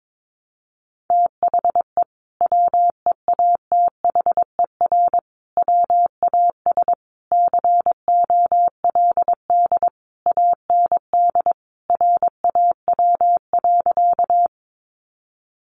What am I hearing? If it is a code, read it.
Morse code, heard as a keyed tone, about 22 wpm, 700 Hz. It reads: THE WEATHER WAS COLD AND RAW.
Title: Sentences from Top 1500 Words 22wpm